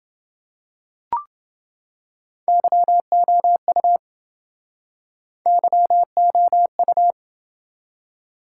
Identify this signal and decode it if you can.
Morse code, heard as a keyed tone, about 27 wpm, 700 Hz. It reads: YOU YOU